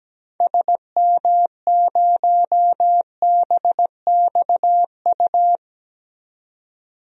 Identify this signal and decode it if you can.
Morse code, heard as a keyed tone, about 17 wpm, 700 Hz. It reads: SM0BXU
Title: DXCC-100 Call Signs - 6 Characters - 17wpm